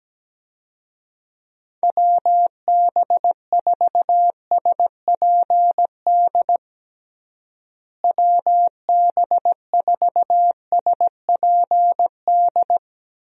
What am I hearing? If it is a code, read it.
Morse code, heard as a keyed tone, about 17 wpm, 700 Hz. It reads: WB4SPD WB4SPD
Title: USA Call Signs - 6 Characters - Repeated 2x - 17wpm